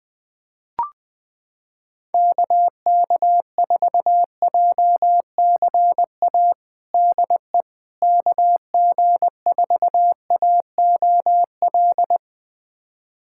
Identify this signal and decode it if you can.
Morse code, heard as a keyed tone, about 21 wpm, 700 Hz. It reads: KK4JCA DE KG4AOL
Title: USA Callsign DE Callsign - 6 Characters - 20wpm